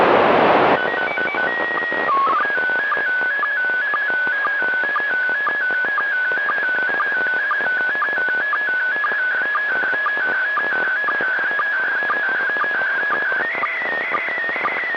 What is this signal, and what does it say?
SSTV (slow-scan television), PD 120.
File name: POEM-4 11h30 UTC by SP5LOT